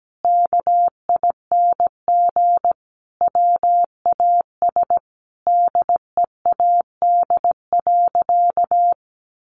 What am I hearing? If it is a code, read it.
Morse code, heard as a keyed tone, about 17 wpm, 700 Hz. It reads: KING WAS DEAD.
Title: Sentences from Top 1500 Words 17wpm